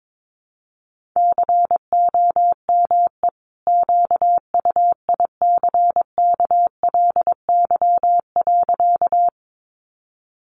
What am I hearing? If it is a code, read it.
Morse code, heard as a keyed tone, about 22 wpm, 700 Hz. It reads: COME QUICKLY.
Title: Sentences from Top 700 Words 22wpm